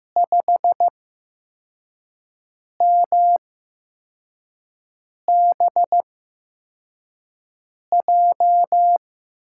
Morse code, heard as a keyed tone, about 15 wpm, spaced as for 4 wpm, 700 Hz. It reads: 5MBJ
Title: USA Call Signs - 5 Characters - 8x Character Spacing - 15wpm